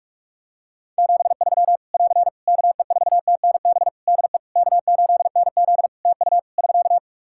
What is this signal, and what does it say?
Morse code, heard as a keyed tone, about 41 wpm, 700 Hz. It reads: 73 <AR> KE4TNB DE K8NZ TU <SK>
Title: Practice QSO - Exchange 4 - Ending - 45wpm